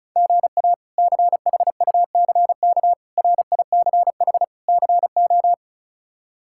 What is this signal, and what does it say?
Morse code, heard as a keyed tone, about 36 wpm, 700 Hz. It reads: GA CHUCK RICH CO
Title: K1USN SST QSO Practice - 35wpm